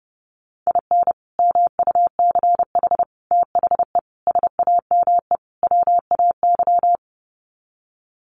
Morse code, heard as a keyed tone, about 30 wpm, 700 Hz. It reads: IN MUCH THE SAME WAY
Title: Sets of 5 Words Encore 30wpm